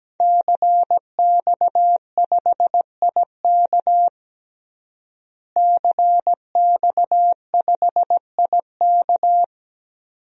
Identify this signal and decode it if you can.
Morse code, heard as a keyed tone, about 17 wpm, 700 Hz. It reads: CX5IK CX5IK
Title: DXCC-100 Call Signs - 5 Characters - Repeated 2x - 17wpm